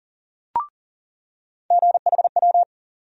Morse code, heard as a keyed tone, about 39 wpm, 700 Hz. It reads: GHW
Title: USA Call Sign Postfixes - 3 Letters 40wpm